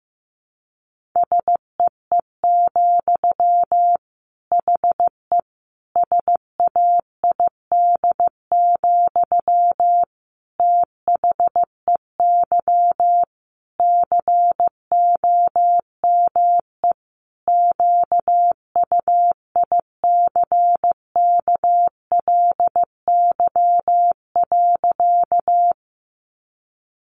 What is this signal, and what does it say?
Morse code, heard as a keyed tone, about 15 wpm, 700 Hz. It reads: SEE, HE SAID, THEY COME QUICKLY.
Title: Sentences from Top 700 Words 15wpm